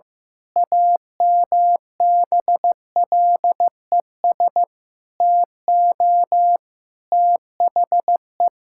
Morse code, heard as a keyed tone, about 15 wpm, 700 Hz. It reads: AMBLES TO THE
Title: Pangram Challenge v2 - 15wpm